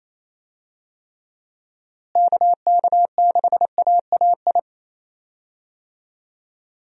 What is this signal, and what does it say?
Morse code, heard as a keyed tone, about 28 wpm, 700 Hz. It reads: KK6AAI
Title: USA Call Signs - 6 Characters - Repeated 3x - 28wpm